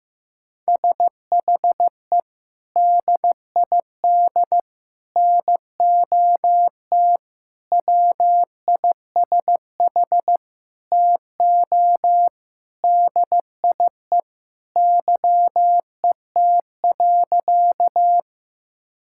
Morse code, heard as a keyed tone, about 15 wpm, 700 Hz. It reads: SHE DID NOT WISH TO DIE YET.
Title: Sentences from Top 1600 Words 15wpm